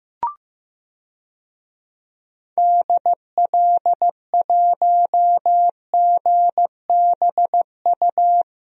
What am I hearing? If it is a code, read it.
Morse code, heard as a keyed tone, about 16 wpm, 700 Hz. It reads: DL1GBU
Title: DXCC-100 Call Signs - 6 Characters - 15wpm